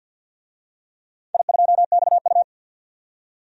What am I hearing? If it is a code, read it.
Morse code, heard as a keyed tone, about 46 wpm, 700 Hz. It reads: I2XU
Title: DXCC-100 Call Signs - 4 Characters - 50wpm